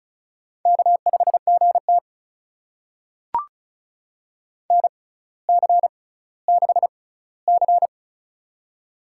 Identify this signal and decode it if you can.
Morse code, heard as a keyed tone, about 36 wpm, spaced as for 18 wpm, 700 Hz. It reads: K5GT NC6C
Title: USA Call Signs - 6x Character Spacing - 35wpm